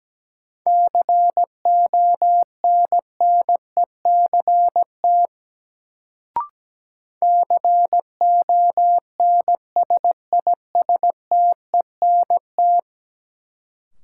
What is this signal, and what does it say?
Morse code, heard as a keyed tone, about 17 wpm, 700 Hz. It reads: CONNECT CONSISTENT
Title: Sentences from Top 1500 Words - New Word Drill - 17wpm